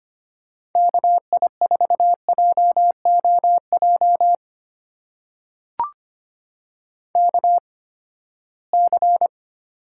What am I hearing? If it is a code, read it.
Morse code, heard as a keyed tone, about 25 wpm, spaced as for 17 wpm, 700 Hz. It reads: KI4JOJ KC
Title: USA Call Signs - 6 Characters - 8x Character Spacing - 25wpm